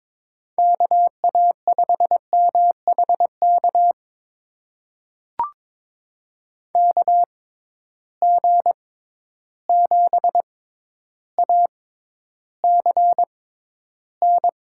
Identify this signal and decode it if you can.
Morse code, heard as a keyed tone, about 22 wpm, spaced as for 11 wpm, 700 Hz. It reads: KA5MHK KG7ACN
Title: USA Call Signs - 6 Characters - 6x Character Spacing - 22wpm